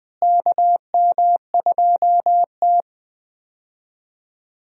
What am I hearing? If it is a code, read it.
Morse code, heard as a keyed tone, about 20 wpm, 700 Hz. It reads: KM2T
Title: USA Call Signs - 4 Characters - Repeated 2x - 20wpm